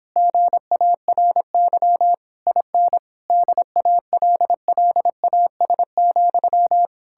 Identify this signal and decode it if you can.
Morse code, heard as a keyed tone, about 29 wpm, 700 Hz. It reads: GARY IN DALLAS,